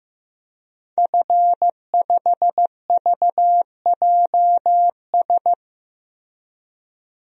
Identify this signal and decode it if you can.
Morse code, heard as a keyed tone, about 15 wpm, 700 Hz. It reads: F5VJS
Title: DXCC-100 Call Signs - 5 Characters - Repeated 2x - 15wpm